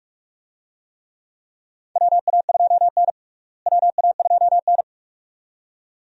Morse code, heard as a keyed tone, about 48 wpm, 700 Hz. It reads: WA2N WA2N
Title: USA Call Signs - 4 Characters - Repeated 2x - 45wpm